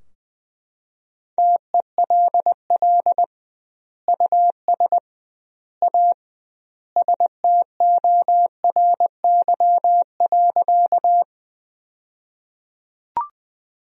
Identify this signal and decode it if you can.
Morse code, heard as a keyed tone, about 20 wpm, 700 Hz. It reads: TELL US A STORY.
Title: 2x Word-Spacing - Sentences from Top 300 Words - 20wpm